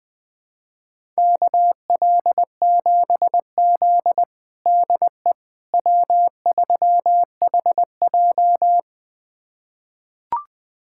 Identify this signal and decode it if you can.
Morse code, heard as a keyed tone, about 21 wpm, 700 Hz. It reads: KL7Z DE W3HJ
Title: USA Callsign DE Callsign - 4 Characters - 20wpm